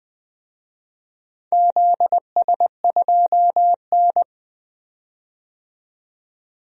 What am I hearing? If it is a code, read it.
Morse code, heard as a keyed tone, about 20 wpm, 700 Hz. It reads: ZS2N
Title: DXCC-100 Call Signs - 4 Characters - 20wpm